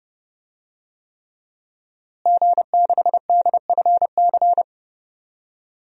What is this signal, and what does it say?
Morse code, heard as a keyed tone, about 31 wpm, 700 Hz. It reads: G6DFC
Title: DXCC-100 Call Signs - 5 Characters - Repeated 3x - 30wpm